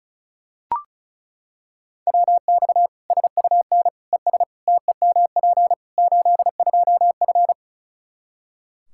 Morse code, heard as a keyed tone, about 34 wpm, 700 Hz. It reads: WX SUN ES TEMP 82F
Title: K1USN SST QSO Practice 35wpm